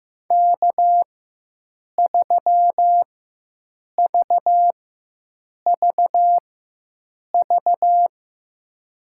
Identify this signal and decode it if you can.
Morse code, heard as a keyed tone, about 15 wpm, spaced as for 7 wpm, 700 Hz. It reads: K3VVV
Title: USA Call Signs - 5 Characters - 4x Character Spacing - 15wpm